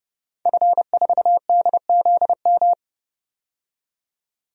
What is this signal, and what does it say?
Morse code, heard as a keyed tone, about 29 wpm, 700 Hz. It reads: F4DZM
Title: DXCC-100 Call Signs - 5 Characters - 30wpm